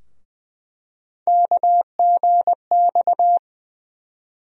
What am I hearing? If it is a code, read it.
Morse code, heard as a keyed tone, about 20 wpm, 700 Hz. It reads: KGX
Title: USA Call Sign Postfixes - 3 Letters 20wpm